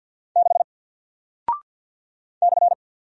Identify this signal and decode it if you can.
Morse code, heard as a keyed tone, about 48 wpm, 700 Hz. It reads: 6 /
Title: Single Character (Letter:Number:Punctuation:Prosign) Alt 50wpm